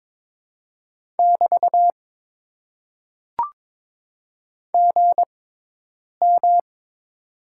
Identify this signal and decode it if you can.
Morse code, heard as a keyed tone, about 22 wpm, spaced as for 8 wpm, 700 Hz. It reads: <BT> GM
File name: Top 90 QSO Elements - 6x Character Spacing - 22wpm